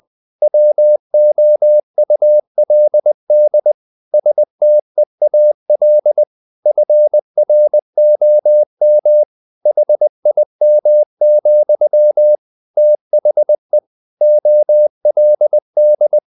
Morse code, heard as a keyed tone, about 20 wpm, 600 Hz. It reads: WOULD STEAL FROM HIM, THE OLD
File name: the old man and the sea / the old man and the sea part 1 20wpm